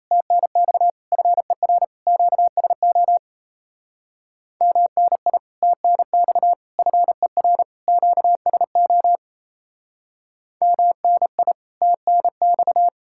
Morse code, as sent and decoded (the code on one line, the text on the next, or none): - -. -..- / ..-. .-. / --.- ... --- / -- -. .. / - -. -..- / ..-. .-. / --.- ... --- / -- -. .. / - -. -..-
TNX FER QSO MNI TNX FER QSO MNI TNX